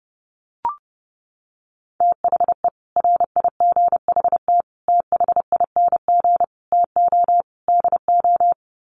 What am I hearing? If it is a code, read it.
Morse code, heard as a keyed tone, about 30 wpm, 700 Hz. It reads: THE RIGHT THING TO DO
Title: Sets of 5 Words 30wpm